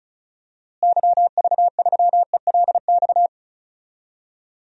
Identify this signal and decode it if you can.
Morse code, heard as a keyed tone, about 36 wpm, 700 Hz. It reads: YV3ELX
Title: DXCC-100 Call Signs - 6 Characters - Repeated 2x - 35wpm